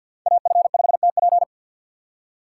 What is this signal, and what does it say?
Morse code, heard as a keyed tone, about 48 wpm, 700 Hz. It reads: AF5TP
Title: USA Call Signs - 5 Characters - 4x Character Spacing - 50wpm